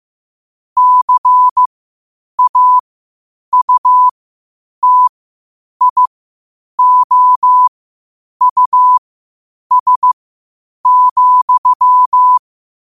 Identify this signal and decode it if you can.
Morse code, heard as a keyed tone, about 15 wpm, spaced as for 8 wpm, 1000 Hz. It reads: CAUTIOUS,
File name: qotdfiles / 8wpm